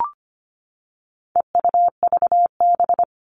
Morse code, heard as a keyed tone, about 25 wpm, 700 Hz. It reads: EUVB
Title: All Cut Numbers - 4-Digit 25wpm